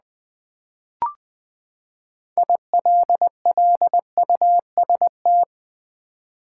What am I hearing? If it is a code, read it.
Morse code, heard as a keyed tone, about 20 wpm, 700 Hz. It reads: ILLUST
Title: Build A Word - 6 Letters At A Time - 12-Letter Words - 20wpm